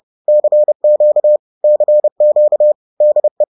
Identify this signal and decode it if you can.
Morse code, heard as a keyed tone, about 30 wpm, 600 Hz. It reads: CQ CQ DE
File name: 30wpm-7